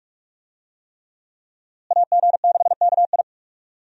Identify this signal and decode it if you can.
Morse code, heard as a keyed tone, about 43 wpm, 700 Hz. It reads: AG6KI
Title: USA Call Signs - 5 Characters - Repeated 3x - 45wpm